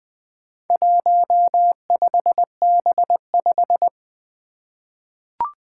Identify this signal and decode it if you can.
Morse code, heard as a keyed tone, about 20 wpm, 700 Hz. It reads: 15B5
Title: Four Challenging Letter-Number Combinations 20wpm